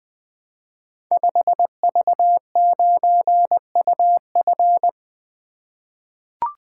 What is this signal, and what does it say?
Morse code, heard as a keyed tone, about 21 wpm, 700 Hz. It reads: 5V9UF
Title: Code Groups - 5 Letter:Numbers - 20wpm